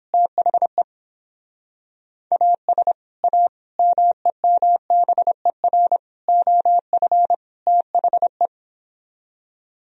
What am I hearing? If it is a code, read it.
Morse code, heard as a keyed tone, about 31 wpm, 700 Hz. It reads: THE AS A MEMBER OF THE